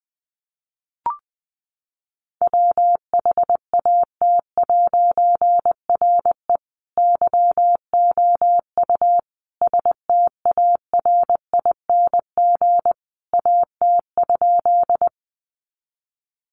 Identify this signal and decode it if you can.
Morse code, heard as a keyed tone, about 20 wpm, 700 Hz. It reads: WHAT'RE YOU STARING AT?
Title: A Sound Of Thunder 20wpm